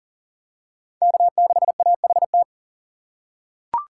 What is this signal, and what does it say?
Morse code, heard as a keyed tone, about 40 wpm, 700 Hz. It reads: K6AHT
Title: USA Call Signs - 5 Characters - 3x Character Spacing - 40wpm